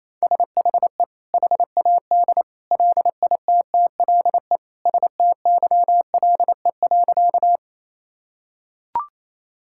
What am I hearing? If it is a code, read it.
Morse code, heard as a keyed tone, about 28 wpm, 700 Hz. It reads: SHE HAD LITTLE STYLE.
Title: Sentences from Top 1000 Words - Review Part 5 - 28wpm